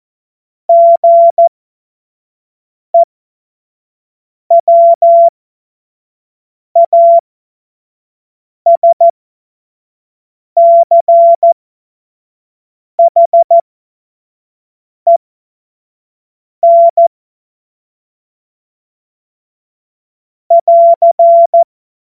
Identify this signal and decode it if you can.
Morse code, heard as a keyed tone, about 14 wpm, spaced as for 5 wpm, 680 Hz. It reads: GEWASCHEN <AR>
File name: Morsekurs / Witze / Witz 24. August 2021, 25 F70